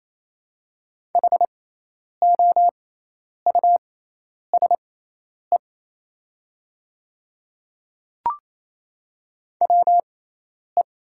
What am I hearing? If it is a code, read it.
Morse code, heard as a keyed tone, about 28 wpm, spaced as for 10 wpm, 700 Hz. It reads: HOUSE WE
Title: 6x Character Spacing - Top 300 Words - 28wpm